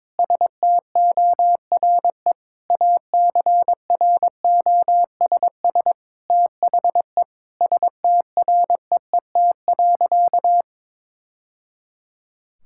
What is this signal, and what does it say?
Morse code, heard as a keyed tone, about 22 wpm, 700 Hz. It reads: STORE ACROSS THE STREET.
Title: Sentences from Top 1000 Words - Review Part 5 - 22wpm